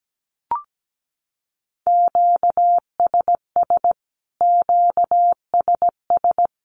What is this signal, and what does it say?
Morse code, heard as a keyed tone, about 17 wpm, 700 Hz. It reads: QSS QSS
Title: Q-Codes Repeated 3x 17wpm